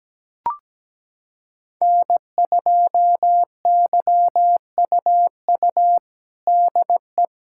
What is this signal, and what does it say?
Morse code, heard as a keyed tone, about 17 wpm, 700 Hz. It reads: N2YUU DE
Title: USA Callsign DE Callsign - 5 Characters - 17wpm